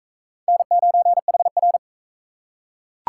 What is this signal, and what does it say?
Morse code, heard as a keyed tone, about 41 wpm, 700 Hz. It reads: N9HR